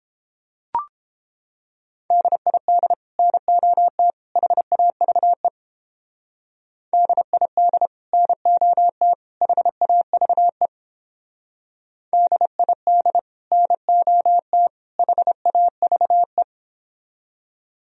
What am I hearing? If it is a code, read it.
Morse code, heard as a keyed tone, about 29 wpm, 700 Hz. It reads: DID NOT HAVE DID NOT HAVE DID NOT HAVE